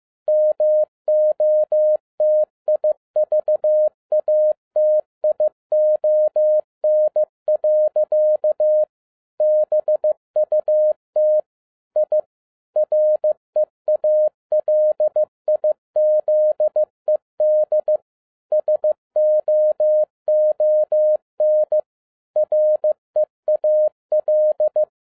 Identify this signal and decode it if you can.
Morse code, heard as a keyed tone, about 15 wpm, 610 Hz. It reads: MOTIVATION. BUT I REALIZED SOON REAL
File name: MY CW LIFE 1 15wpm